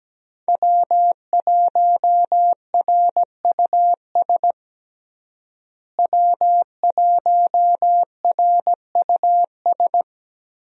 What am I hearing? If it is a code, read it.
Morse code, heard as a keyed tone, about 17 wpm, 700 Hz. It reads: W1RUS W1RUS
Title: USA Call Signs - 5 Characters - Repeated 2x - 17wpm